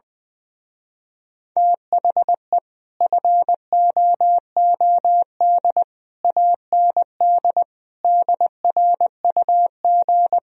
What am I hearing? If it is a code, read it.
Morse code, heard as a keyed tone, about 20 wpm, 700 Hz. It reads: THE FOOD AND DRUG
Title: Speed-Racing - Sets of 5 Words 17wpm